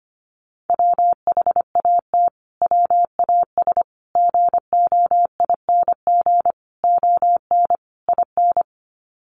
Morse code, heard as a keyed tone, about 25 wpm, 700 Hz. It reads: WHAT WAS GOING ON IN
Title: Sets of 5 Words Encore 25wpm